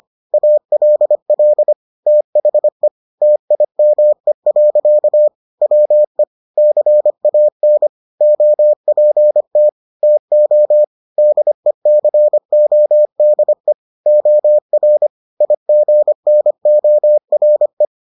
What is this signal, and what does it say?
Morse code, heard as a keyed tone, about 25 wpm, 600 Hz. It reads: ALL THE TIME. WE CAN OPT TO DECODE OR IGNORE